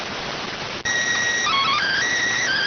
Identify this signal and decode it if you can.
SSTV (slow-scan television), PD 290.